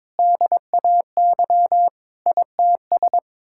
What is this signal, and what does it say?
Morse code, heard as a keyed tone, about 21 wpm, 700 Hz. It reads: DAY ITS
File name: practice / 22wpm / Sets of 8 Words - 22wpm